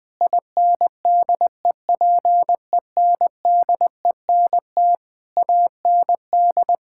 Morse code, heard as a keyed tone, about 21 wpm, 700 Hz. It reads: INDEPENDENT AND
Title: Sentences from Top 1000 Words - Review Part 6 - 20wpm